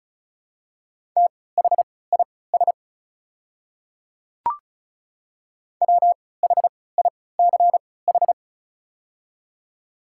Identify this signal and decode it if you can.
Morse code, heard as a keyed tone, about 34 wpm, spaced as for 20 wpm, 690 Hz. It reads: THIS WHICH
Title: 3x Character Spacing - Top 300 Words - 35wpm